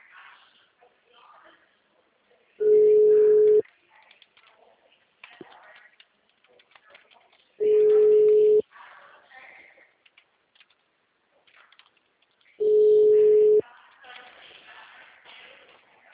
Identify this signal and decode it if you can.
Ringback tone (Europe, 425 Hz), with speech.